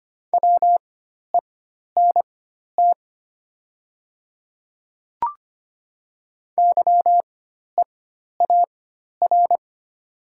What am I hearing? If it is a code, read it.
Morse code, heard as a keyed tone, about 25 wpm, spaced as for 12 wpm, 700 Hz. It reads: WENT YEAR
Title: 4x Character Spacing - Top 200 Words - 25wpm